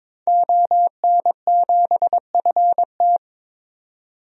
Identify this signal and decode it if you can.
Morse code, heard as a keyed tone, about 21 wpm, 700 Hz. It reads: ON7FT
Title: DXCC-100 Call Signs - 5 Characters - 22wpm